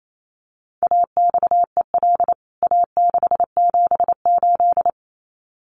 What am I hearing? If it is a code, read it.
Morse code, heard as a keyed tone, about 28 wpm, 700 Hz. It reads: AXEL A678
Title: CWT Exchange 28wpm